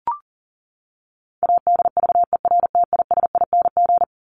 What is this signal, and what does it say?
Morse code, heard as a keyed tone, about 40 wpm, 700 Hz. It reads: ADVERTISING
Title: Top 751-1000 Words 40wpm